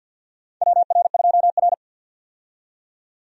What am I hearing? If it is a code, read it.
Morse code, heard as a keyed tone, about 52 wpm, 700 Hz. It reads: WR2R
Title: USA Call Signs - 4 Characters - 50wpm